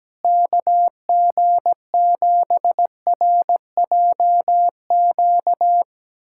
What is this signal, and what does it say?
Morse code, heard as a keyed tone, about 17 wpm, 700 Hz. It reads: KG7RJQ
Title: USA Callsign DE Callsign 17wpm